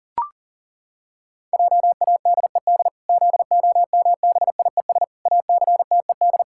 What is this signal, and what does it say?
Morse code, heard as a keyed tone, about 40 wpm, 700 Hz. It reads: JADED ZOMBIES ACTED
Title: Pangram Challenge v2 - 40wpm